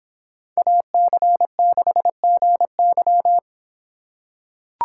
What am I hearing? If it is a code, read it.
Morse code, heard as a keyed tone, about 25 wpm, 700 Hz. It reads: AC6GY